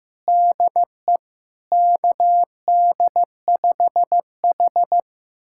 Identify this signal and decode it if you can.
Morse code, heard as a keyed tone, about 15 wpm, 700 Hz. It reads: DE KD5H
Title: USA Callsign DE Callsign - 4 Characters - 15wpm